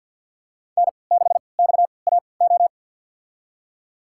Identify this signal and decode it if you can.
Morse code, heard as a keyed tone, about 49 wpm, spaced as for 28 wpm, 700 Hz. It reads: N6XAO
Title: USA Call Signs - 5 Characters - 3x Character Spacing - 50wpm